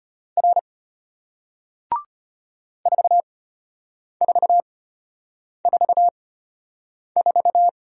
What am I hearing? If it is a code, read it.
Morse code, heard as a keyed tone, about 33 wpm, spaced as for 8 wpm, 700 Hz. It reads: R 4444